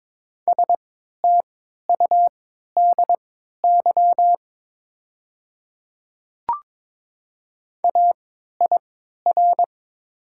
Morse code, heard as a keyed tone, about 22 wpm, spaced as for 13 wpm, 700 Hz. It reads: STUDY AIR
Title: 3x Character Spacing - Top 300 Words - 22wpm